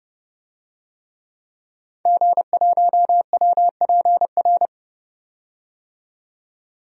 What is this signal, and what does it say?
Morse code, heard as a keyed tone, about 32 wpm, 700 Hz. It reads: G1WPR
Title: DXCC-100 Call Signs - 5 Characters - 30wpm